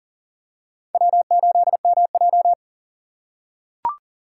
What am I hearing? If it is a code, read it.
Morse code, heard as a keyed tone, about 40 wpm, 700 Hz. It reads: W8MJ
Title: USA Call Signs - 3x Character Spacing - 40wpm